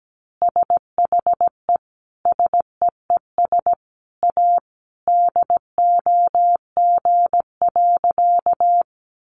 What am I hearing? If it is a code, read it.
Morse code, heard as a keyed tone, about 17 wpm, 700 Hz. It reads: SHE SEES A DOG.
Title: Easy Phrases 17wpm